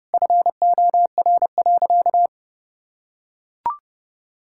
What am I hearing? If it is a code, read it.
Morse code, heard as a keyed tone, about 32 wpm, 700 Hz. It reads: FOR.
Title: Sentences from Top 1000 Words - Review Part 6 - 30wpm